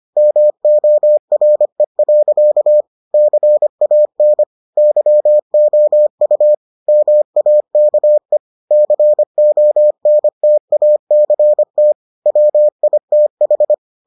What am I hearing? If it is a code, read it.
Morse code, heard as a keyed tone, about 25 wpm, 600 Hz. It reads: MORE. CAN YOU MAKE CONTACT WITH